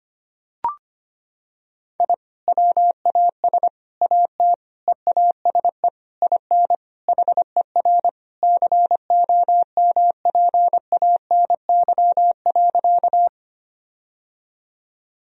Morse code, heard as a keyed tone, about 25 wpm, 700 Hz. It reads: I WAS AT EASE IN HER COMPANY.
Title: Sentences from Top 1000 Words - Review Part 1 - 25wpm